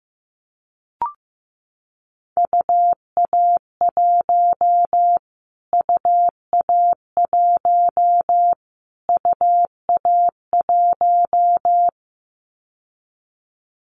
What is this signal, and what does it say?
Morse code, heard as a keyed tone, about 15 wpm, 700 Hz. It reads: UA1 UA1 UA1
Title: Callsign Prefixes for Top 25 DXCC - Repeated 3x - 15wpm